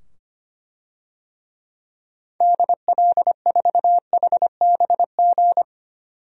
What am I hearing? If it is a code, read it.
Morse code, heard as a keyed tone, about 25 wpm, 700 Hz. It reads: DL4HBG
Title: DXCC-100 Call Signs - 6 Characters - 25wpm